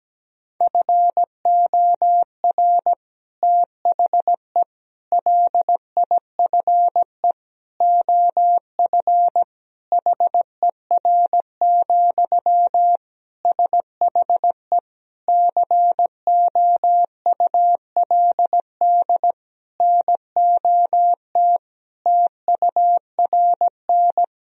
Morse code, heard as a keyed tone, about 17 wpm, 700 Hz. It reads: FOR THE LIFE OF HER, SHE COULD NOT TURN
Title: Sentences from Top 1000 Words - Review Part 2 - 17wpm